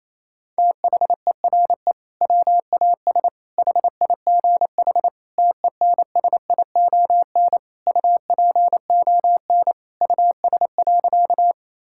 Morse code, heard as a keyed tone, about 28 wpm, 700 Hz. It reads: THERE WAS HIGH TENSION UPON US.
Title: Sentences from Top 1600 Words 28wpm